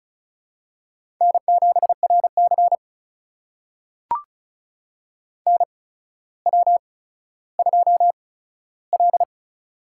Morse code, heard as a keyed tone, about 36 wpm, spaced as for 15 wpm, 700 Hz. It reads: N7RC NW2L
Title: USA Call Signs - 4 Characters - 8x Character Spacing - 35wpm